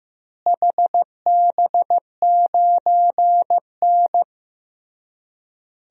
Morse code, heard as a keyed tone, about 15 wpm, 700 Hz. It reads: HB9N
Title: DXCC-100 Call Signs - 4 Characters - Repeated 3x - 15wpm